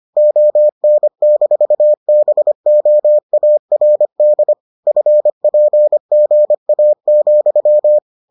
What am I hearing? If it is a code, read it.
Morse code, heard as a keyed tone, about 25 wpm, 600 Hz. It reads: ON-BOARD FPGA,